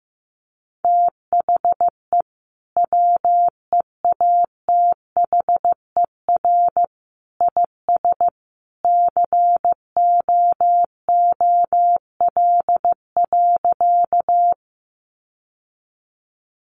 Morse code, heard as a keyed tone, about 15 wpm, 700 Hz. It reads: THE WEATHER IS COOL.
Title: Sentences from Top 1200 Words 15wpm